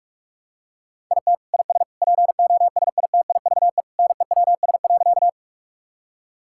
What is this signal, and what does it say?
Morse code, heard as a keyed tone, about 46 wpm, 700 Hz. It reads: IT IS POSITIVE NEWS.
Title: Sentences from Top 1000 Words - Review Part 5 - 45wpm